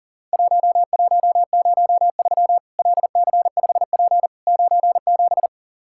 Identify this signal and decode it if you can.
Morse code, heard as a keyed tone, about 40 wpm, 700 Hz. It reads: 1103 LC5P 97
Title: Worked All Europe DX Contest - QTC Exchange - 40wpm